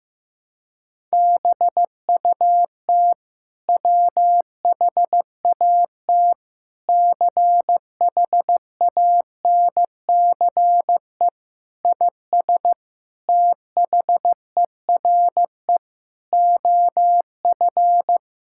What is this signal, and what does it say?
Morse code, heard as a keyed tone, about 15 wpm, 700 Hz. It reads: BUT WHAT CHANCE IS THERE OF
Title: Sentences from Top 1000 Words - Review Part 6 - 15wpm